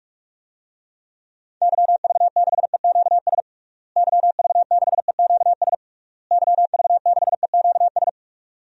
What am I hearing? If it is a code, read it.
Morse code, heard as a keyed tone, about 46 wpm, 700 Hz. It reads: YV6EQS YV6EQS YV6EQS
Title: DXCC-100 Call Signs - 6 Characters - Repeated 3x - 45wpm